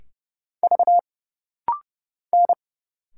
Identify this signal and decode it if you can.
Morse code, heard as a keyed tone, about 30 wpm, 700 Hz. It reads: V N
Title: Single Letters - Rapid-Fire 30wpm